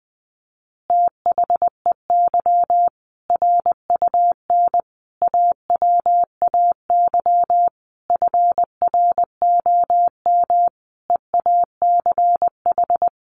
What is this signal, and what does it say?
Morse code, heard as a keyed tone, about 20 wpm, 700 Hz. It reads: THEY RUN AWAY FROM EACH
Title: Easy Phrases 20wpm